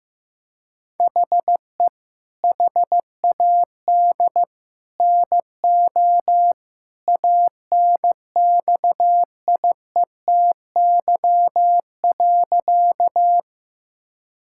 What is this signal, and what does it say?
Morse code, heard as a keyed tone, about 15 wpm, 700 Hz. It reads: HE HAD NO ANXIETY.
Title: Sentences from Top 1600 Words 15wpm